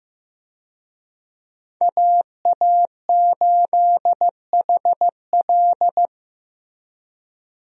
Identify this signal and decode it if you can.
Morse code, heard as a keyed tone, about 15 wpm, 700 Hz. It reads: AA8HL
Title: USA Call Signs - 5 Characters - Repeated 3x - 15wpm